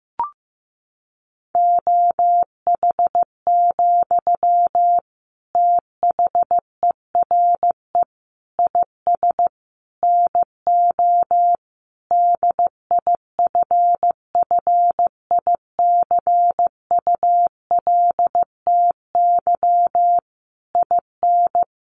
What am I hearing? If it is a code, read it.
Morse code, heard as a keyed tone, about 15 wpm, 700 Hz. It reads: OH, THERE IS NO DIFFICULTY IN
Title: Sentences from Top 1500 Words 15wpm